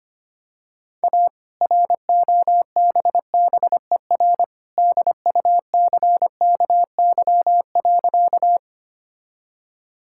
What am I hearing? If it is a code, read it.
Morse code, heard as a keyed tone, about 25 wpm, 700 Hz. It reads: A ROBBER DUCKY.
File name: practice / 25wpm / All-Age Appropriate Jokes - 25wpm